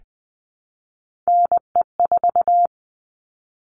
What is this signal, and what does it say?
Morse code, heard as a keyed tone, about 20 wpm, 700 Hz. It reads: NE4
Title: All Cut Numbers - 3-Digit 20wpm